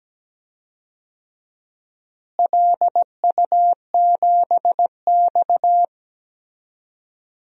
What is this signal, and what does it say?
Morse code, heard as a keyed tone, about 17 wpm, 700 Hz. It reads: LU7X
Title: DXCC-100 Call Signs - 4 Characters - Repeated 3x - 17wpm